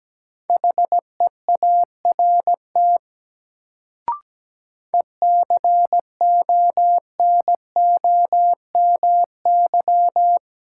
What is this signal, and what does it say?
Morse code, heard as a keyed tone, about 17 wpm, 700 Hz. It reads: HEART ECONOMY
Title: Sentences from Top 1200 Words - New Word Drill Alt - 17wpm